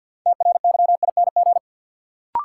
Morse code, heard as a keyed tone, about 52 wpm, 700 Hz. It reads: TRYING
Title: Sets of 2 Words - New Word Drill - 50wpm